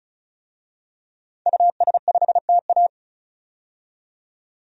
Morse code, heard as a keyed tone, about 34 wpm, 700 Hz. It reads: US5TA
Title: DXCC-100 Call Signs - 5 Characters - 35wpm